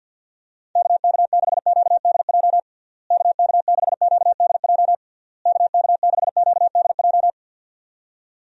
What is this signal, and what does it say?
Morse code, heard as a keyed tone, about 51 wpm, 700 Hz. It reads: KK6QDJ KK6QDJ KK6QDJ
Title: USA Call Signs - 6 Characters - Repeated 3x - 50wpm